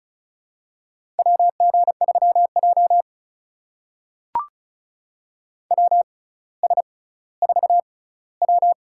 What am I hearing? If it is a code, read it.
Morse code, heard as a keyed tone, about 36 wpm, spaced as for 18 wpm, 700 Hz. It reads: WG3J WS4W
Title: USA Call Signs - 4 Characters - 6x Character Spacing - 35wpm